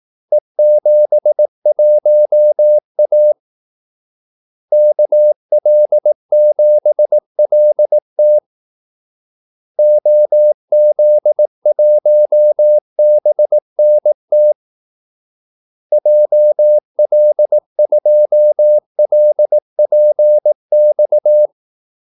Morse code, heard as a keyed tone, about 18 wpm, 600 Hz. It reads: E71A KL7LT OZ1BNT JL2LPX
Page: Kaldesignaler 18 wpm